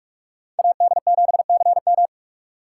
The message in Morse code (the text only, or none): AD7CM